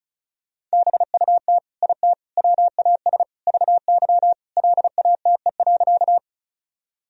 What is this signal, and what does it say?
Morse code, heard as a keyed tone, about 34 wpm, 700 Hz. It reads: BUT IT WAS VY LATE.
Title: Sentences from Top 300 Words and Abbreviations 35wpm